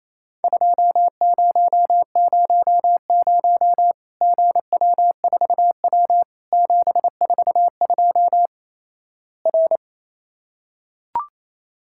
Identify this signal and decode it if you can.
Morse code, heard as a keyed tone, about 28 wpm, 700 Hz. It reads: 2000 GW4W 742 R
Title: Worked All Europe DX Contest - QTC Exchange - 28wpm